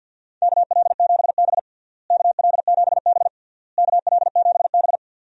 Morse code, heard as a keyed tone, about 51 wpm, 700 Hz. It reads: KL7B KL7B KL7B
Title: DXCC-100 Call Signs - 4 Characters - Repeated 3x - 50wpm